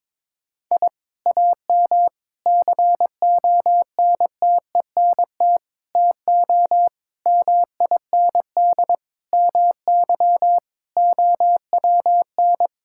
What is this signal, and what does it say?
Morse code, heard as a keyed tone, about 22 wpm, 700 Hz. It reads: I AM CONTENT TO MIND MY OWN
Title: Sentences from Top 1000 Words - Review Part 6 - 22wpm